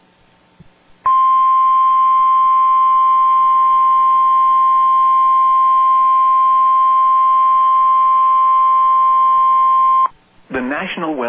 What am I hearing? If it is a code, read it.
Emergency Alert System (SAME) tones (the 1050 Hz attention tone), with speech. It is heard from 1.0 to 10.1 s.